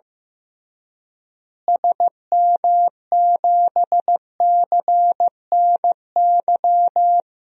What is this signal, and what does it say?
Morse code, heard as a keyed tone, about 15 wpm, 700 Hz. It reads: SM7CNY
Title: DXCC-100 Call Signs - 6 Characters - Repeated 3x - 15wpm